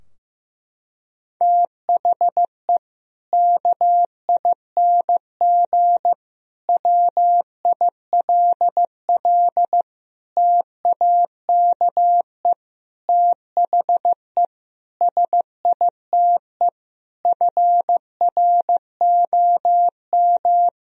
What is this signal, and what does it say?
Morse code, heard as a keyed tone, about 15 wpm, 700 Hz. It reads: THE KING WILL TAKE THE SITE FROM
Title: Sentences from Top 1000 Words - Review Part 4 - 15wpm